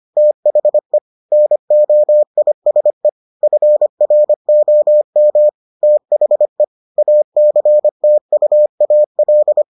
Morse code, heard as a keyed tone, about 25 wpm, 600 Hz. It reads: THE NOISE FROM THE ACTUAL